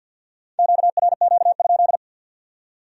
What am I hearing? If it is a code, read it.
Morse code, heard as a keyed tone, about 61 wpm, 700 Hz. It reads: QRQ?
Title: Speed-Racing Q-Codes - 50wpm